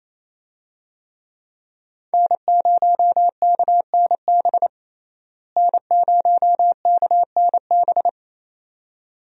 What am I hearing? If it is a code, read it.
Morse code, heard as a keyed tone, about 28 wpm, 700 Hz. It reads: N0KNB N0KNB
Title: USA Call Signs - 5 Characters - Repeated 2x - 28wpm